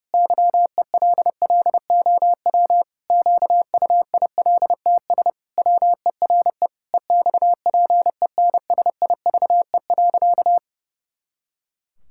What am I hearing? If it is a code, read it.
Morse code, heard as a keyed tone, about 32 wpm, 700 Hz. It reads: YELLOW QUILTS WERE EXPENSIVE.
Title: Pangram Challenge v2 - 30wpm